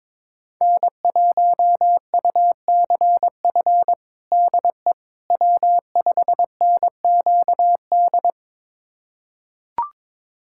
Morse code, heard as a keyed tone, about 21 wpm, 700 Hz. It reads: N1UCF DE W5NQD
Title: USA Callsign DE Callsign - 5 Characters - 22wpm